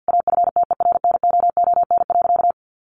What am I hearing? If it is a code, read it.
Morse code, heard as a keyed tone, about 50 wpm, 700 Hz. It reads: AFTERNOON.
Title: Sentences from Top 1500 Words 50wpm